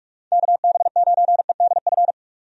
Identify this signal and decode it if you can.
Morse code, heard as a keyed tone, about 46 wpm, 700 Hz. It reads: KB9EDF
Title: USA Call Signs - 6 Characters - 45wpm